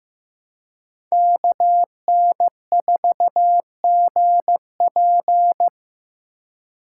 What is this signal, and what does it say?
Morse code, heard as a keyed tone, about 16 wpm, 700 Hz. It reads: KN4GP
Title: USA Call Signs - 4x Character Spacing - 15wpm